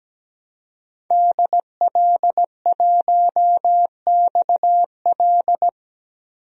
Morse code, heard as a keyed tone, about 17 wpm, 700 Hz. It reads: DL1XL
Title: DXCC-100 Call Signs - 5 Characters - 17wpm